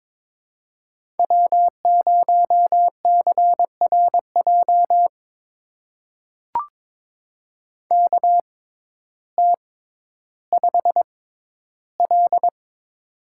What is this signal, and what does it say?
Morse code, heard as a keyed tone, about 22 wpm, spaced as for 12 wpm, 700 Hz. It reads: W0CRJ KT5L
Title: USA Call Signs - 6x Character Spacing - 22wpm